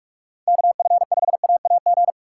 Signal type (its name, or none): Morse code